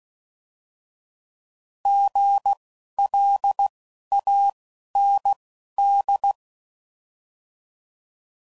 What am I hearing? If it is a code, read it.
Morse code, heard as a keyed tone, about 16 wpm, spaced as for 11 wpm, 800 Hz. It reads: GLAND
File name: Prefixes / By First Letter